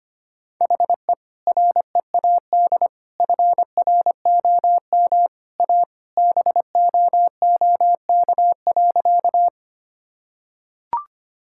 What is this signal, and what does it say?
Morse code, heard as a keyed tone, about 25 wpm, 700 Hz. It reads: HE READ FROM A BOOK.
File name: practice / 25wpm / Sentences with 2-Word Phrases - 25wpm